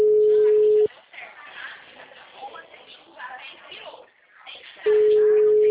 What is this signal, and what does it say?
Ringback tone (Europe, 425 Hz), with speech.